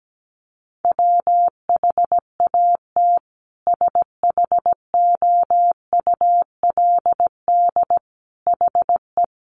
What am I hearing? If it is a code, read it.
Morse code, heard as a keyed tone, about 17 wpm, 700 Hz. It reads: WHAT SHOULD HE
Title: Sentences from Top 300 Words 17wpm